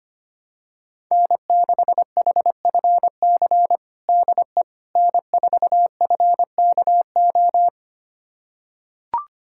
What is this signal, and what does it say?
Morse code, heard as a keyed tone, about 25 wpm, 700 Hz. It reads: N6HFC DE N4FKO
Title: USA Callsign DE Callsign - 5 Characters - 25wpm